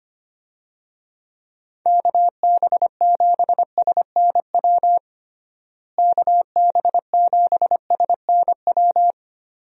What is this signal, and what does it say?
Morse code, heard as a keyed tone, about 25 wpm, 700 Hz. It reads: KB7SNW KB7SNW
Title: USA Call Signs - 6 Characters - Repeated 3x - 25wpm